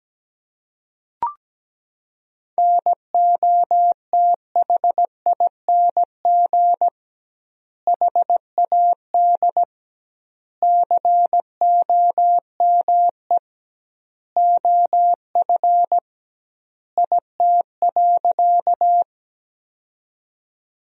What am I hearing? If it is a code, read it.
Morse code, heard as a keyed tone, about 17 wpm, 700 Hz. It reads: NOTHING HAD COME OF IT.
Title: 2x Word-Spacing - Sentences from Top 500 Words - 17wpm